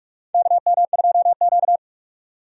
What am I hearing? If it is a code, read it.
Morse code, heard as a keyed tone, about 46 wpm, 700 Hz. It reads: KM2Q
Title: USA Call Signs - 4 Characters - 45wpm